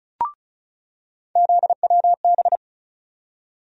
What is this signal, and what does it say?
Morse code, heard as a keyed tone, about 34 wpm, 700 Hz. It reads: ZWB
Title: USA Call Sign Postfixes - 3 Letters 35wpm